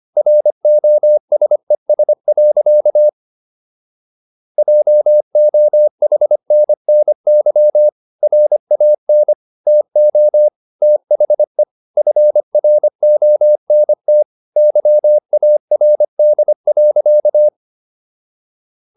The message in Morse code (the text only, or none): ROSES. JOHNNY RAN TO THE FRONT YARD.